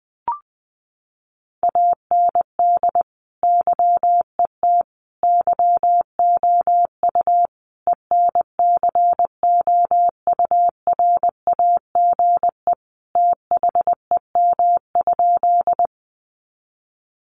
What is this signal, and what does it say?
Morse code, heard as a keyed tone, about 20 wpm, 700 Hz. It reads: AND YET YOU ENCOURAGE THEM?
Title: Sentences from Top 1400 Words 20wpm